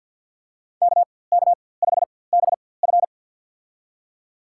Morse code, heard as a keyed tone, about 52 wpm, spaced as for 24 wpm, 700 Hz. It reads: KK5BF
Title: USA Call Signs - 5 Characters - 4x Character Spacing - 50wpm